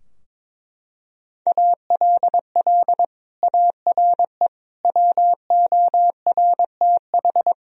Morse code, heard as a keyed tone, about 22 wpm, 700 Hz. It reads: ALL ARE WORTH
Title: Sentences from Top 1000 Words - Review Part 3 - 22wpm